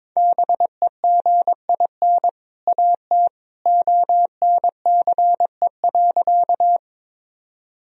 Morse code, heard as a keyed tone, about 22 wpm, 700 Hz. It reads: BEGIN AT ONCE.
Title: Sentences from Top 1000 Words - Review Part 1 - 22wpm